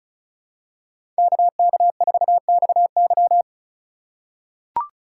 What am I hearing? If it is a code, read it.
Morse code, heard as a keyed tone, about 36 wpm, 700 Hz. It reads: KK4XY
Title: USA Call Signs - 3x Character Spacing - 35wpm